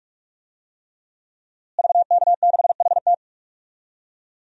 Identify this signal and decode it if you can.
Morse code, heard as a keyed tone, about 46 wpm, 700 Hz. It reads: VK6HT
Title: DXCC-100 Call Signs - 5 Characters - Repeated 3x - 45wpm